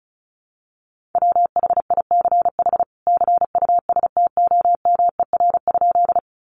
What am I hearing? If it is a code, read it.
Morse code, heard as a keyed tone, about 34 wpm, 700 Hz. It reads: WHICH CUSTOMER?
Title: Sentences from Top 1100 Words 35wpm